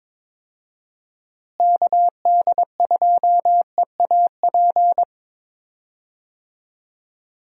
Morse code, heard as a keyed tone, about 22 wpm, 700 Hz. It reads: KD2EAP
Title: USA Call Signs - 6 Characters - Repeated 2x - 22wpm